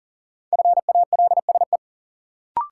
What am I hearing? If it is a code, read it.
Morse code, heard as a keyed tone, about 40 wpm, 700 Hz. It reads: FALSE
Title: Sentences from Top 1600 Words - New Word Drill Alt - 40wpm